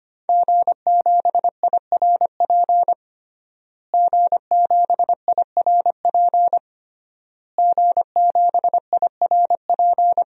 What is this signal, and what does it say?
Morse code, heard as a keyed tone, about 25 wpm, 700 Hz. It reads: G7IRP G7IRP G7IRP
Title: DXCC-100 Call Signs - 5 Characters - Repeated 3x - 25wpm